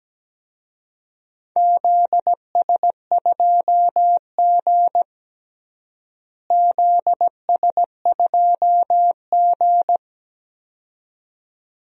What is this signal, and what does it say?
Morse code, heard as a keyed tone, about 17 wpm, 700 Hz. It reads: ZS2G ZS2G
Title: DXCC-100 Call Signs - 4 Characters - Repeated 2x - 17wpm